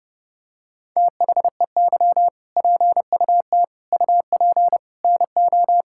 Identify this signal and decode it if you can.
Morse code, heard as a keyed tone, about 30 wpm, 700 Hz. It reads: THEY PUT UP NO
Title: Sentences from Top 1600 Words 30wpm